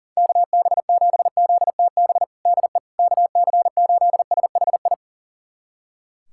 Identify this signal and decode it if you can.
Morse code, heard as a keyed tone, about 40 wpm, 700 Hz. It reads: KB7ZTB DE KC8SHI
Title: USA Callsign DE Callsign - 6 Characters - 40wpm